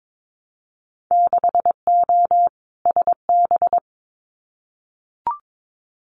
Morse code, heard as a keyed tone, about 22 wpm, 700 Hz. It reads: 6O SB
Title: Winter Field Day Exchanges Encore 22wpm